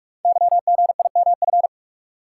Morse code, heard as a keyed tone, about 44 wpm, 700 Hz. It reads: YGIMF
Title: Code Groups - 5 Letter:Numbers - 45wpm